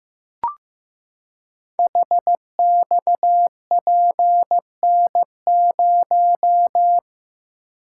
Morse code, heard as a keyed tone, about 15 wpm, 700 Hz. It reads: HXPN0
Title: Code Groups - 5 Letter:Numbers - 15wpm